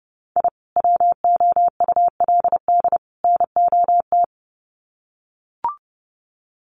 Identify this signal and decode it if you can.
Morse code, heard as a keyed tone, about 30 wpm, 700 Hz. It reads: I WOULD NOT
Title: Sets of 3 Words 30wpm